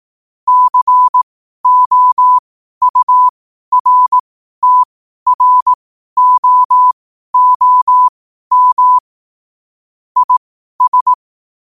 Morse code, heard as a keyed tone, about 18 wpm, spaced as for 12 wpm, 1000 Hz. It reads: COURTROOM IS
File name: qotdfiles / 12wpm